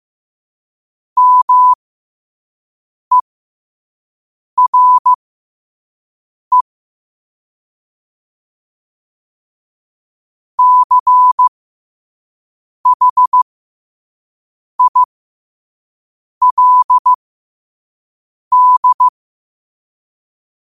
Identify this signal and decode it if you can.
Morse code, heard as a keyed tone, about 15 wpm, spaced as for 5 wpm, 1000 Hz. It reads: MERE CHILD
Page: New quotes every day in morse code at 5 Words per minute.